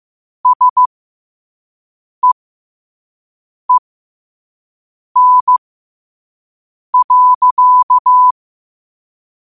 Morse code, heard as a keyed tone, about 15 wpm, spaced as for 5 wpm, 1000 Hz. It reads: SEEN.
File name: qotdfiles / 5wpm